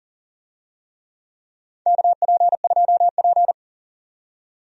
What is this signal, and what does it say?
Morse code, heard as a keyed tone, about 37 wpm, 700 Hz. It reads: KP2P
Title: DXCC-100 Call Signs - 4 Characters - 40wpm